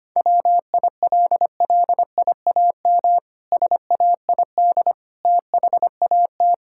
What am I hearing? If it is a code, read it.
Morse code, heard as a keyed tone, about 25 wpm, 700 Hz. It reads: WILLIAM SAID THAT
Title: Pangram Challenge v2 - 25wpm